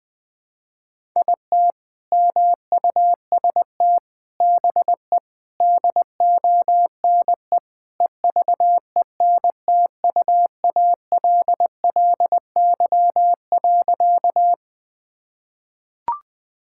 Morse code, heard as a keyed tone, about 20 wpm, 700 Hz. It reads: IT MUST BE DONE EVENTUALLY.
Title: Sentences from Top 1000 Words - Review Part 6 - 20wpm